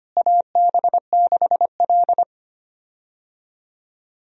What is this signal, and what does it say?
Morse code, heard as a keyed tone, about 25 wpm, 700 Hz. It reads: AB6L
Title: USA Call Signs - 4 Characters - Repeated 3x - 25wpm